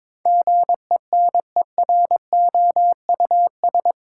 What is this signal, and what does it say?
Morse code, heard as a keyed tone, about 21 wpm, 700 Hz. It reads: GENEROUS
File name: practice / 22wpm / Eight Letter Words - 22wpm